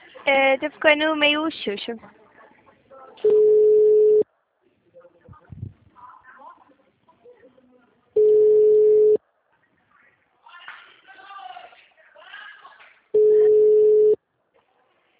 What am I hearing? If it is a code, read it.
Ringback tone (Europe, 425 Hz), with speech.